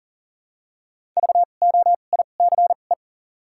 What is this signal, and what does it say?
Morse code, heard as a keyed tone, about 40 wpm, spaced as for 29 wpm, 700 Hz. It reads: VOICE
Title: 2x Character Spacing - Top 500 Words - 40wpm